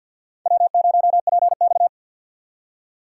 Morse code, heard as a keyed tone, about 64 wpm, 700 Hz. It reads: W0PX